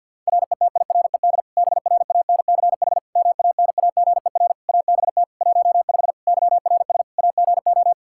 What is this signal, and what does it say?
Morse code, heard as a keyed tone, about 51 wpm, 700 Hz. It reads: RETIRED BRANCH MANAGER ABT 15 YRS AGO
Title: Practice QSO - Exchange 3 - More Info - 50wpm